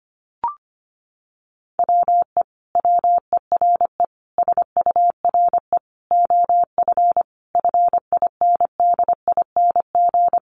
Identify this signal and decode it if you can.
Morse code, heard as a keyed tone, about 25 wpm, 700 Hz. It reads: WE WERE SURE OF FINDING
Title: Sentences from Top 1100 Words 25wpm